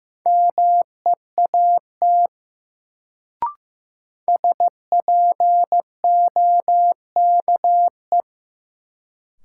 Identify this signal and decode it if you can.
Morse code, heard as a keyed tone, about 15 wpm, 700 Hz. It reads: MEAT SPOKE
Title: Sentences from Top 1200 Words - New Word Drill Alt - 15wpm